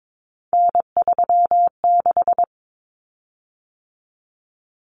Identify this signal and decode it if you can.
Morse code, heard as a keyed tone, about 22 wpm, 700 Hz. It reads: N36
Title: Common Cut Numbers - 3-Digit 22wpm